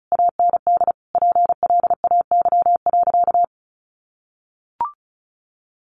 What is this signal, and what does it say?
Morse code, heard as a keyed tone, about 34 wpm, 700 Hz. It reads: AND PLAY.
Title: Sentences from Top 1300 Words 35wpm